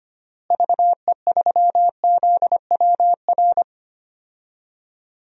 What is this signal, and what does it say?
Morse code, heard as a keyed tone, about 25 wpm, 700 Hz. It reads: VE3ZWR
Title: DXCC-100 Call Signs - 6 Characters - Repeated 3x - 25wpm